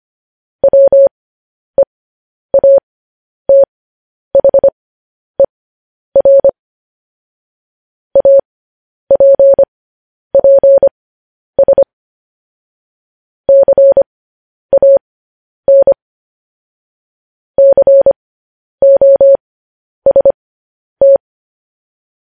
Morse code, heard as a keyed tone, about 25 wpm, spaced as for 10 wpm, 550 Hz. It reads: WEATHER APPS CAN COST
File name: podcast / News headlines-1x-25x10